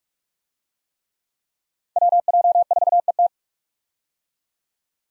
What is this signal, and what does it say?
Morse code, heard as a keyed tone, about 41 wpm, 700 Hz. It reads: WJ4ET
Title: USA Call Signs - 5 Characters - 45wpm